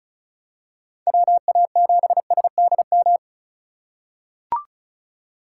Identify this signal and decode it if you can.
Morse code, heard as a keyed tone, about 35 wpm, 690 Hz. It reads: WA7SDM
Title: USA Call Signs - 6 Characters - 3x Character Spacing - 35wpm